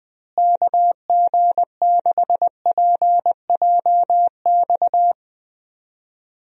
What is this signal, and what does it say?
Morse code, heard as a keyed tone, about 21 wpm, 700 Hz. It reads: KG6PJX
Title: USA Call Signs - 6 Characters - Repeated 2x - 20wpm